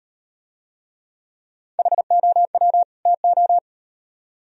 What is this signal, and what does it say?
Morse code, heard as a keyed tone, about 38 wpm, 700 Hz. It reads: HOW TO